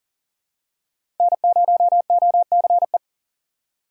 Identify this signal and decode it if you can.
Morse code, heard as a keyed tone, about 38 wpm, 700 Hz. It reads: N0OCE
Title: USA Call Signs - 3x Character Spacing - 40wpm